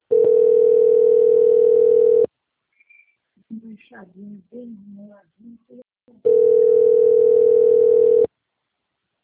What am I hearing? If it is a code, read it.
Ringback tone (North America), 440 + 480 Hz, with speech.
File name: out- - - - - .40113